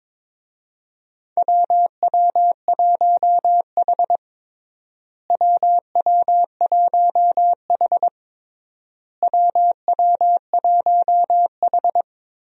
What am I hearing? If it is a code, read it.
Morse code, heard as a keyed tone, about 22 wpm, 700 Hz. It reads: WW1H WW1H WW1H
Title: USA Call Signs - 4 Characters - Repeated 3x - 22wpm